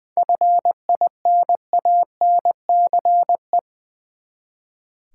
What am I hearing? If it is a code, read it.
Morse code, heard as a keyed tone, about 20 wpm, 700 Hz. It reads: FINANCE
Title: Seven Letter Words - 20wpm